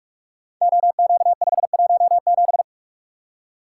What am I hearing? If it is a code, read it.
Morse code, heard as a keyed tone, about 44 wpm, 700 Hz. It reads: OQ517
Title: Code Groups - 5 Letter:Numbers - 45wpm